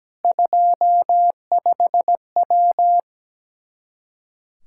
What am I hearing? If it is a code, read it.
Morse code, heard as a keyed tone, about 17 wpm, 700 Hz. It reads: 25W
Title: Practice QSO - Exchange 2 - Info - 17wpm